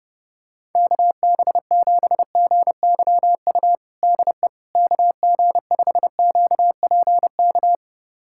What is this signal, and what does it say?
Morse code, heard as a keyed tone, about 32 wpm, 700 Hz. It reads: KB7GYU DE KG5QPK
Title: USA Callsign DE Callsign - 6 Characters - 30wpm